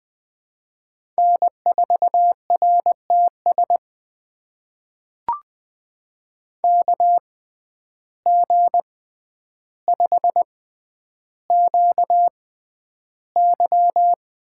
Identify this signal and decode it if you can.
Morse code, heard as a keyed tone, about 20 wpm, spaced as for 10 wpm, 700 Hz. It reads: N4RTS KG5QY
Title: USA Call Signs - 5 Characters - 6x Character Spacing - 20wpm